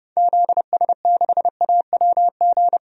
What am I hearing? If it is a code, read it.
Morse code, heard as a keyed tone, about 30 wpm, 700 Hz. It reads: ZS6AWG
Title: DXCC-100 Call Signs - 6 Characters - 30wpm